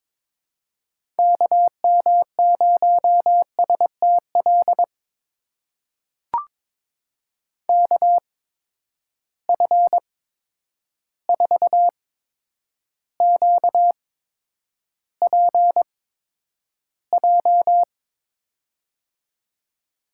Morse code, heard as a keyed tone, about 22 wpm, spaced as for 9 wpm, 700 Hz. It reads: KM0STL KF4QPJ
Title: USA Call Signs - 6 Characters - 8x Character Spacing - 22wpm